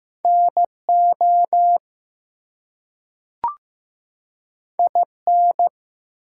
Morse code, heard as a keyed tone, about 15 wpm, 700 Hz. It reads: NO IN
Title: Two Letter Words - 15wpm